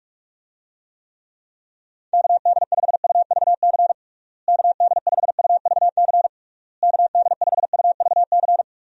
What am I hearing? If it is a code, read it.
Morse code, heard as a keyed tone, about 46 wpm, 700 Hz. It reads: KD5UVC KD5UVC KD5UVC
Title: USA Call Signs - 6 Characters - Repeated 3x - 45wpm